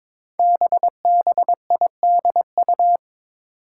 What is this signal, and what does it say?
Morse code, heard as a keyed tone, about 22 wpm, 700 Hz. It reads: BBIDU
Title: Five Challenging Letter-Number Combinations 22wpm